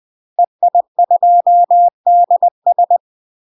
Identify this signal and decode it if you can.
Morse code, heard as a keyed tone, about 20 wpm, 700 Hz. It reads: EI2DS
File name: UK MASTER QSOs TRAINING / 20WPM / UK MASTER QSOs 0001